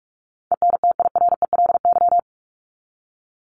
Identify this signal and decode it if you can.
Morse code, heard as a keyed tone, about 44 wpm, 700 Hz. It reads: ENTIRELY
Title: LY Suffix - Top 100 Words with Base Words 45wpm